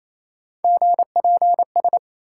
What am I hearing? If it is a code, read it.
Morse code, heard as a keyed tone, about 28 wpm, 700 Hz. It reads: GPS